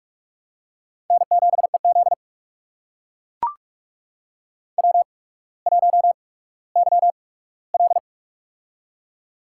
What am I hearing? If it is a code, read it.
Morse code, heard as a keyed tone, about 44 wpm, spaced as for 19 wpm, 700 Hz. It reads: N7EZ W1YL
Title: USA Call Signs - 4 Characters - 8x Character Spacing - 45wpm